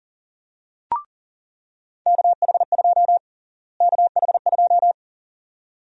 Morse code, heard as a keyed tone, about 40 wpm, 700 Hz. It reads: KH2 KH2
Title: USA Call Sign Prefixes - Repeated 2x - 40wpm